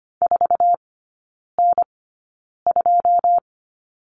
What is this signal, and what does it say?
Morse code, heard as a keyed tone, about 25 wpm, spaced as for 9 wpm, 690 Hz. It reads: 4N2
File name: Lesson 14e - Number 2 - Three Characters 25wpm